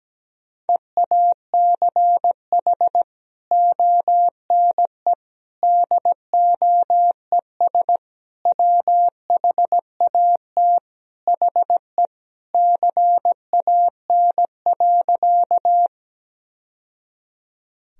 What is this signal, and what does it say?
Morse code, heard as a keyed tone, about 17 wpm, 700 Hz. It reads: EACH ONE DOES WHAT HE CAN.
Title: Sentences from Top 1000 Words - Review Part 1 - 17wpm